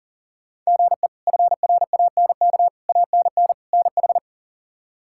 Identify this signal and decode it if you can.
Morse code, heard as a keyed tone, about 40 wpm, 700 Hz. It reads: GE FRANK ANN NH
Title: K1USN SST QSO Practice - 40wpm